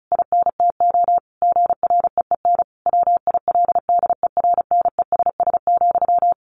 Morse code, heard as a keyed tone, about 34 wpm, 700 Hz. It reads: INTO GREEN WILDERNESS,
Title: A Sound Of Thunder 35wpm